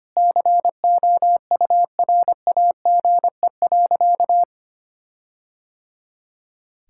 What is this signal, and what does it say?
Morse code, heard as a keyed tone, about 25 wpm, 700 Hz. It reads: COURAGE.
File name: practice / 25wpm / Sentences with 4-Word Phrases - 25wpm